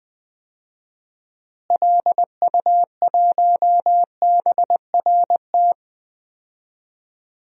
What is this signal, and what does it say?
Morse code, heard as a keyed tone, about 20 wpm, 700 Hz. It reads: LU1BRT
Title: DXCC-100 Call Signs - 6 Characters - Repeated 3x - 20wpm